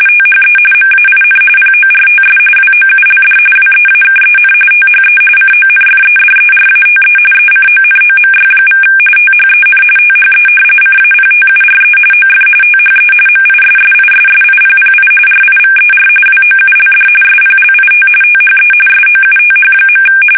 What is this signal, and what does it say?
RTTY, about 45.45 baud. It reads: THE MAIN PORTS WOULD WEAKEN THE CONFEDERATE ECONOMY= THEN THE CAPTURE OF THE MISSISSIPPI RIVER WOULD SPLIT THE SOUTH.
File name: fsk850 / usnavy3